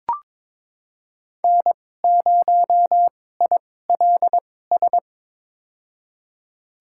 Morse code, heard as a keyed tone, about 22 wpm, spaced as for 16 wpm, 700 Hz. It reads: N0ILS
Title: USA Call Signs - 5 Characters - 2x Character Spacing - 22wpm